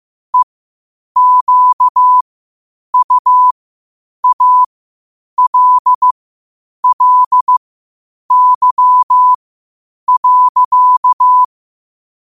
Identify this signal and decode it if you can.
Morse code, heard as a keyed tone, about 15 wpm, spaced as for 8 wpm, 1000 Hz. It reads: EQUALLY.
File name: qotdfiles / 8wpm